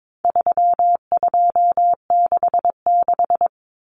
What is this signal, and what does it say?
Morse code, heard as a keyed tone, about 22 wpm, 700 Hz. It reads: 3266
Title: Numbers - 4-Digit 22wpm